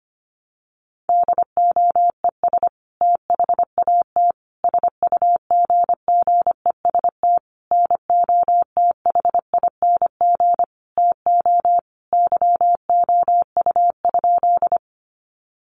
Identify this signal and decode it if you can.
Morse code, heard as a keyed tone, about 25 wpm, 700 Hz. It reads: DOES THAT SUGGEST NOTHING TO YOU?
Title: Sentences from Top 1300 Words 25wpm